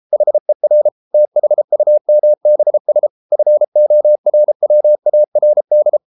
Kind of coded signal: Morse code